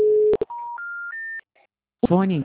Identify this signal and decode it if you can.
Special information tone, with speech.
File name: out- -2005